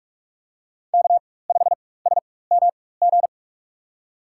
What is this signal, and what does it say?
Morse code, heard as a keyed tone, about 46 wpm, spaced as for 21 wpm, 700 Hz. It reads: K5SMG
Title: USA Call Signs - 5 Characters - 4x Character Spacing - 45wpm